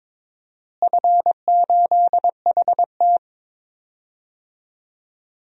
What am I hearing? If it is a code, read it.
Morse code, heard as a keyed tone, about 22 wpm, 700 Hz. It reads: F8HT
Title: DXCC-100 Call Signs - 4 Characters - Repeated 2x - 22wpm